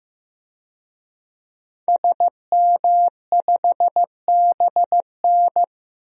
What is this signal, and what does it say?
Morse code, heard as a keyed tone, about 15 wpm, 700 Hz. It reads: SM5BN
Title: DXCC-100 Call Signs - 5 Characters - 15wpm